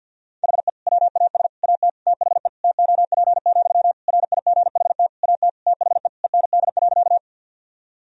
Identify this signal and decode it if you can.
Morse code, heard as a keyed tone, about 51 wpm, 700 Hz. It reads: HE WAS AT THE TOP, RIGHT AT THE END.
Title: Sentences from Top 1000 Words - Review Part 2 - 50wpm